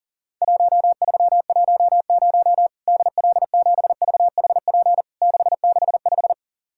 Morse code, heard as a keyed tone, about 40 wpm, 700 Hz. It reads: 1310 DL7VHP 665
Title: Worked All Europe DX Contest - QTC Exchange - 40wpm